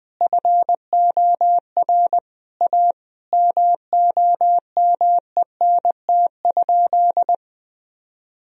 Morse code, heard as a keyed tone, about 20 wpm, 700 Hz. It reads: FOR A MOMENT?
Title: Sentences from Top 1000 Words - Review Part 1 - 20wpm